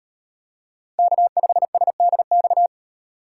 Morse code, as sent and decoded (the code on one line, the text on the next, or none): -.- ..... ... -.. -..-
K5SDX